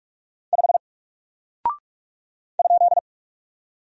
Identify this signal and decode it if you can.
Morse code, heard as a keyed tone, about 46 wpm, 700 Hz. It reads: 5 ?
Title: Single Character (Letter:Number:Punctuation:Prosign) 45wpm